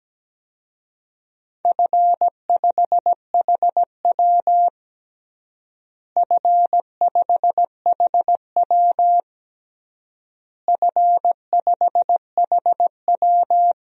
Morse code, heard as a keyed tone, about 17 wpm, 700 Hz. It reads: F5HW F5HW F5HW
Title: DXCC-100 Call Signs - 4 Characters - Repeated 3x - 17wpm